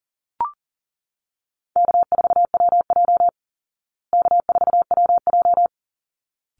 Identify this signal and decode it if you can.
Morse code, heard as a keyed tone, about 40 wpm, 700 Hz. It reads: K4WJ K4WJ
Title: Easy Call Signs Repeated 2x - 40wpm